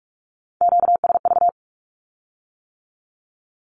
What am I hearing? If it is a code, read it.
Morse code, heard as a keyed tone, about 46 wpm, 700 Hz. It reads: QSV
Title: Q-Codes with Spoken Meaning - 45wpm